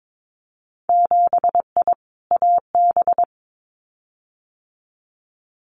Morse code, heard as a keyed tone, about 22 wpm, 700 Hz. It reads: 7I AB
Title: Winter Field Day Exchanges Encore 22wpm